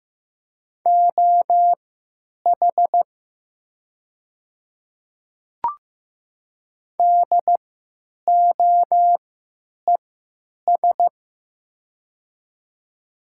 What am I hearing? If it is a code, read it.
Morse code, heard as a keyed tone, about 15 wpm, spaced as for 9 wpm, 700 Hz. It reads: OH DOES
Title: 3x Character Spacing - Top 500 Words - 15wpm